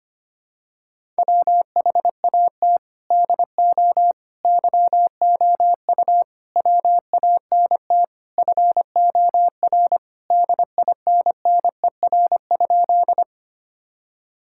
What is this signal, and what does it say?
Morse code, heard as a keyed tone, about 25 wpm, 700 Hz. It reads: WHAT DO YOU WANT FOR DINNER?
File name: practice / 25wpm / Sentences with 4-Word Phrases - 25wpm